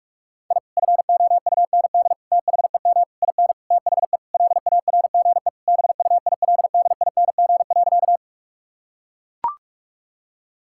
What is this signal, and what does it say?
Morse code, heard as a keyed tone, about 41 wpm, 700 Hz. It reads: I FOUND THEM IN THE LARGE BUILDING.
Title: Sentences from Top 1000 Words - Review Part 5 - 45wpm